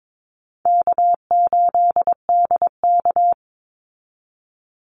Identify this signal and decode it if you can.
Morse code, heard as a keyed tone, about 22 wpm, 700 Hz. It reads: K8DK
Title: Easy Call Signs Repeated 3x - 22wpm